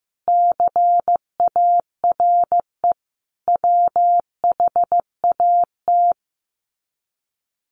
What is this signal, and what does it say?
Morse code, heard as a keyed tone, about 15 wpm, 700 Hz. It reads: CARE WHAT
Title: Sets of 5 Words 15wpm